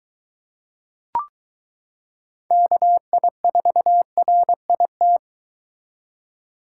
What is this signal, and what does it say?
Morse code, heard as a keyed tone, about 23 wpm, 700 Hz. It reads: KI4RIT